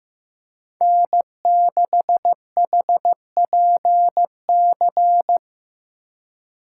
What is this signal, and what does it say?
Morse code, heard as a keyed tone, about 16 wpm, 700 Hz. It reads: N6HPC
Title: USA Call Signs - 5 Characters - 2x Character Spacing - 15wpm